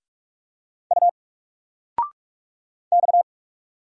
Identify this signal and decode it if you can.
Morse code, heard as a keyed tone, about 46 wpm, 700 Hz. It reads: U X
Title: Single Letter-Number 45wpm